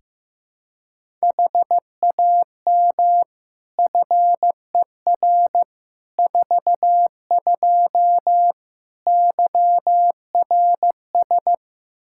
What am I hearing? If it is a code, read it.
Morse code, heard as a keyed tone, about 15 wpm, 700 Hz. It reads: HAM FER 42 YRS
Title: Practice QSO - Exchange 3 - More Info - 15wpm